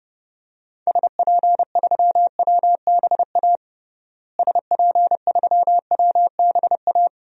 Morse code, heard as a keyed tone, about 30 wpm, 700 Hz. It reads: SP3WBA SP3WBA
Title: DXCC-100 Call Signs - 6 Characters - Repeated 2x - 30wpm